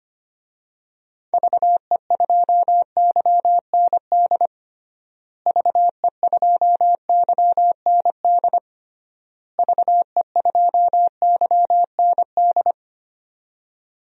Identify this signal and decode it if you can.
Morse code, heard as a keyed tone, about 25 wpm, 700 Hz. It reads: VE2YND VE2YND VE2YND
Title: DXCC-100 Call Signs - 6 Characters - Repeated 3x - 25wpm